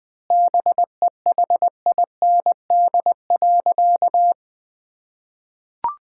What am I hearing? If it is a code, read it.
Morse code, heard as a keyed tone, about 21 wpm, 700 Hz. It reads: BEHIND.
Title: Sentences from Top 1600 Words 20wpm